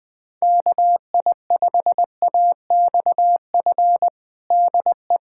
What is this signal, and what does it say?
Morse code, heard as a keyed tone, about 21 wpm, 700 Hz. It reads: KI5AXF DE
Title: USA Callsign DE Callsign - 6 Characters - 20wpm